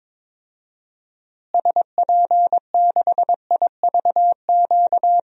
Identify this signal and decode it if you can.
Morse code, heard as a keyed tone, about 22 wpm, 700 Hz. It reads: SP6IVQ
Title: DXCC-100 Call Signs - 6 Characters - Repeated 2x - 22wpm